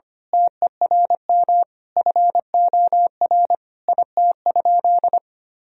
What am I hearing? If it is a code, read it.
Morse code, heard as a keyed tone, about 25 wpm, 690 Hz. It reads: TERM FOR IT?
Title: Sentences from Top 1000 Words - Review Part 4 - 25wpm